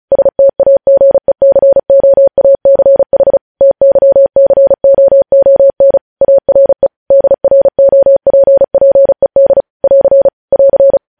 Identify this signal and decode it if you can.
Morse code, heard as a keyed tone, about 36 wpm, 550 Hz. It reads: STAGECOACH TYCOON ARE DROPPED <AR> <AR>